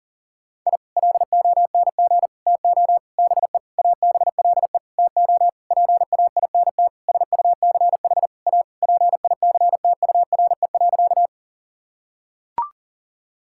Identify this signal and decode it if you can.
Morse code, heard as a keyed tone, about 37 wpm, 700 Hz. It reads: I LONG TO BE ABLE TO PAINT SUCH A PICTURE.
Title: Sentences from Top 1000 Words - Review Part 6 - 40wpm